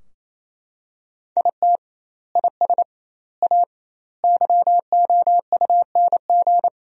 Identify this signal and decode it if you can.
Morse code, heard as a keyed tone, about 28 wpm, 700 Hz. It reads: IT IS A YOUNG
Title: 2x Word-Spacing - Sentences from Top 400 Words - 28wpm